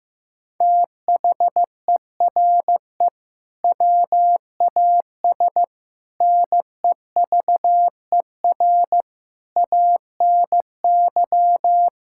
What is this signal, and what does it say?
Morse code, heard as a keyed tone, about 16 wpm, 700 Hz. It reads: THERE WAS NEVER ANY
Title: Sentences from Top 1000 Words - Review Part 6 - 15wpm